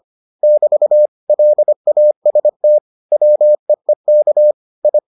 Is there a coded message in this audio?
Yes: Morse code from 0.4 to 5.0 s, heard as a keyed tone, about 25 wpm, 600 Hz. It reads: <BT> LAST WEEK I